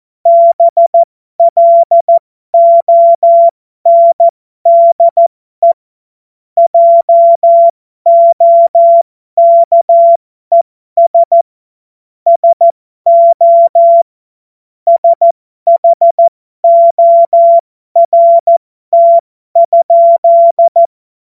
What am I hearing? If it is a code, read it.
Morse code, heard as a keyed tone, about 14 wpm, spaced as for 12 wpm, 680 Hz. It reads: BLONDE JOKES SO SHORT?
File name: Morsekurs / Witze